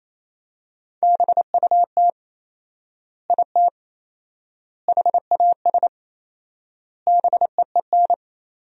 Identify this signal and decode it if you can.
Morse code, heard as a keyed tone, about 27 wpm, 700 Hz. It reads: BUT IT HAS BEEN
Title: 4x Word-Spacing - Sentences from Top 100 Words - 28wpm